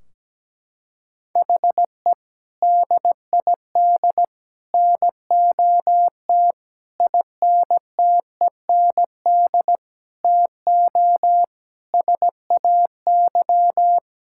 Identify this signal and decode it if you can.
Morse code, heard as a keyed tone, about 17 wpm, 700 Hz. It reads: HE DID NOT INTEND TO SAY
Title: Sentences from Top 1600 Words 17wpm